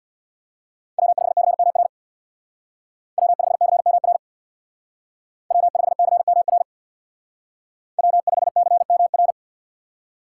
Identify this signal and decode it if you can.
Morse code, heard as a keyed tone, about 61 wpm, 700 Hz. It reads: W5CMR W5CMR W5CMR W5CMR
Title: Speed Racing - USA 1x3 Call Signs 50wpm